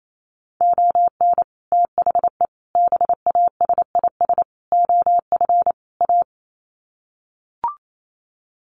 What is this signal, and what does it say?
Morse code, heard as a keyed tone, about 28 wpm, 700 Hz. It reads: ON THE BASIS OF A
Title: Sets of 5 Words Encore 28wpm